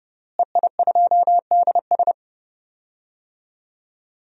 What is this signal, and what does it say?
Morse code, heard as a keyed tone, about 29 wpm, 700 Hz. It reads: EI2DS
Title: DXCC-100 Call Signs - 5 Characters - Repeated 3x - 30wpm